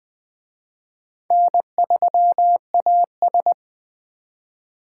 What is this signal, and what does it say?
Morse code, heard as a keyed tone, about 20 wpm, 700 Hz. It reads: N3AS